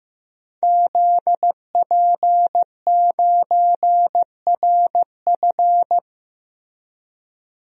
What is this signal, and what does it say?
Morse code, heard as a keyed tone, about 15 wpm, 700 Hz. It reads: ZP9RF
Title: DXCC-100 Call Signs - 5 Characters - Repeated 3x - 15wpm